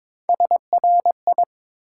Morse code, heard as a keyed tone, about 22 wpm, 700 Hz. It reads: SRI
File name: Top 90 QSO Elements - 8x Character Spacing - 22wpm